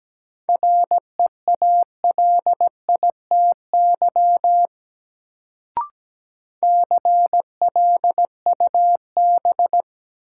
Morse code, heard as a keyed tone, about 17 wpm, 700 Hz. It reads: REALITY CLUB
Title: Sentences from Top 1200 Words - New Word Drill Alt - 17wpm